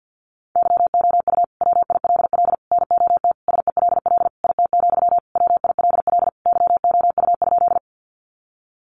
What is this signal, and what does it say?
Morse code, heard as a keyed tone, about 50 wpm, 700 Hz. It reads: YOU WILL NOT SELL IT, WILL YOU?
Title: Sentences from Top 1100 Words 50wpm